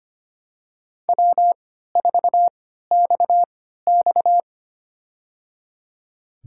Morse code, heard as a keyed tone, about 25 wpm, spaced as for 14 wpm, 700 Hz. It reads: W4XX
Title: USA Call Signs - 3x Character Spacing - 25wpm